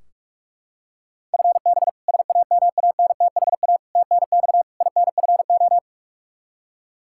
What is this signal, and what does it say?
Morse code, heard as a keyed tone, about 42 wpm, 700 Hz. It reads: FB SAMANTHA TNX INFO
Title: K1USN SST QSO Practice 45wpm